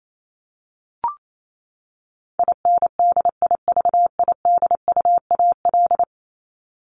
Morse code, heard as a keyed tone, about 28 wpm, 700 Hz. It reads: INDIVIDUAL
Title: ES:S Suffix - Top 100 Words with Base Words 28wpm